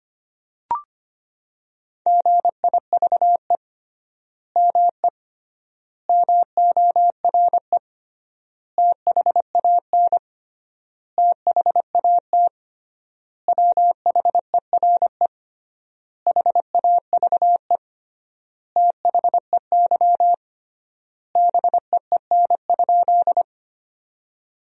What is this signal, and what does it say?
Morse code, heard as a keyed tone, about 25 wpm, 700 Hz. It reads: GIVE ME MORE THAN THAT WHERE HAVE THEY BEEN?
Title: 3x Word-Spacing - Sentences from Top 200 Words - 25wpm